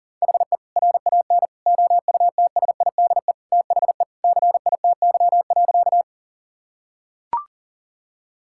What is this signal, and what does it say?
Morse code, heard as a keyed tone, about 38 wpm, 700 Hz. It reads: HE RAN OUTSIDE THE CITY.
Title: Sentences from Top 1000 Words - Review Part 6 - 40wpm